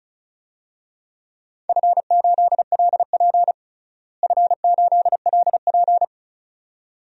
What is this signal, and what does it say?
Morse code, heard as a keyed tone, about 34 wpm, 700 Hz. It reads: F8LP F8LP
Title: DXCC-100 Call Signs - 4 Characters - Repeated 2x - 35wpm